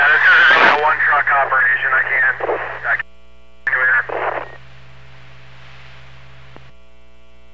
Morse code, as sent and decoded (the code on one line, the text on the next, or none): none